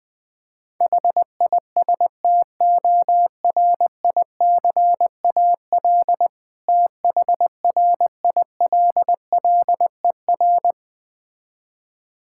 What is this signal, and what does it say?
Morse code, heard as a keyed tone, about 21 wpm, 690 Hz. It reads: HISTORICAL THRILLER
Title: Difficult Words with Hints - 20wpm